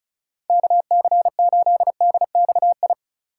Morse code, heard as a keyed tone, about 46 wpm, 700 Hz. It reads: KC8DXI